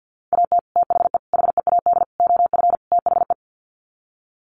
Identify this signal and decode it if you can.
Morse code, heard as a keyed tone, about 50 wpm, 700 Hz. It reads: AT THE HEAD OF THE
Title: Sets of 5 Words 50wpm